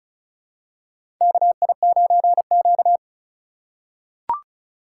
Morse code, heard as a keyed tone, about 35 wpm, 700 Hz. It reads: KI9Q
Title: USA Call Signs - 4 Characters - 3x Character Spacing - 35wpm